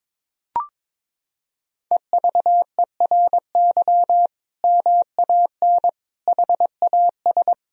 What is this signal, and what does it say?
Morse code, heard as a keyed tone, about 22 wpm, 700 Hz. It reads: EVERY MAN HAS
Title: Sentences from Top 1000 Words - Review Part 4 - 22wpm